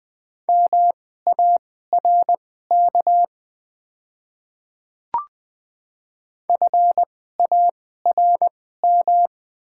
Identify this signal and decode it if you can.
Morse code, heard as a keyed tone, about 20 wpm, spaced as for 14 wpm, 690 Hz. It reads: MARK FARM
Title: 2x Character Spacing - Top 500 Words - 20wpm